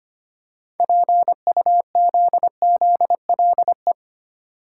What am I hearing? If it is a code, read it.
Morse code, heard as a keyed tone, about 25 wpm, 700 Hz. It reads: PUZZLE
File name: practice / 25wpm / Difficult Words with Hints - 25wpm